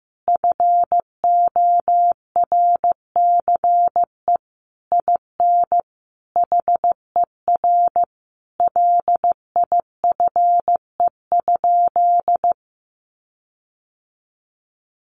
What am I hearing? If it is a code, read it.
Morse code, heard as a keyed tone, about 15 wpm, 700 Hz. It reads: FORCE IN HER LIFE?
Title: Sentences from Top 1500 Words 15wpm